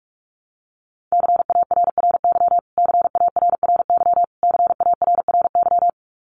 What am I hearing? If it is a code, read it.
Morse code, heard as a keyed tone, about 44 wpm, 700 Hz. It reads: CARRY CARRY CARRY
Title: Top 500 Words Repeated 45wpm